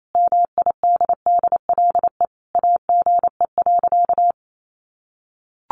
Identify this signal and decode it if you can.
Morse code, heard as a keyed tone, about 28 wpm, 700 Hz. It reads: MIDDLE AGE.
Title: Sentences from Top 1400 Words 28wpm